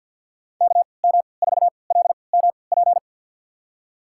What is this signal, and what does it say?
Morse code, heard as a keyed tone, about 53 wpm, spaced as for 29 wpm, 700 Hz. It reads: KM4LMP
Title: USA Call Signs - 6 Characters - 3x Character Spacing - 50wpm